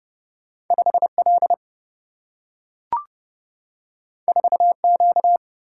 Morse code, heard as a keyed tone, about 30 wpm, 700 Hz. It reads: 5 L 4 Q
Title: Single-Letter Single-Number 30wpm